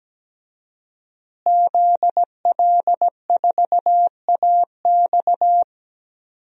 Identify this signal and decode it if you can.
Morse code, heard as a keyed tone, about 17 wpm, 700 Hz. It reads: ZL4AX
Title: DXCC-100 Call Signs - 5 Characters - Repeated 3x - 17wpm